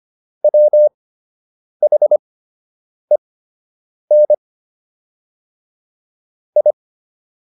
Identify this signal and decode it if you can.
Morse code, heard as a keyed tone, about 25 wpm, spaced as for 8 wpm, 600 Hz. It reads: WHEN I